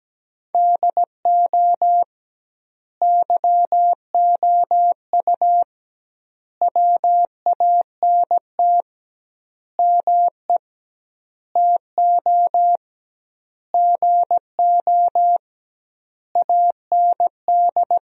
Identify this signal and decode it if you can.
Morse code, heard as a keyed tone, about 17 wpm, 700 Hz. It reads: DO YOU WANT ME TO GO AND
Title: 2x Word-Spacing - Sentences from Top 200 Words - 17wpm